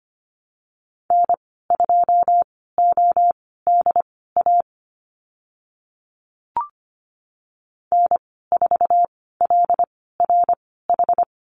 Morse code, heard as a keyed tone, about 25 wpm, spaced as for 16 wpm, 700 Hz. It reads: N2ODA N4LRH
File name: Lesson 15g - Number 5 - Call Signs 25wpm